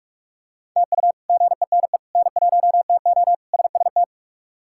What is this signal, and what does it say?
Morse code, heard as a keyed tone, about 46 wpm, 700 Hz. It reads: TU GENE N1TO SST
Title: K1USN SST QSO Practice - 45wpm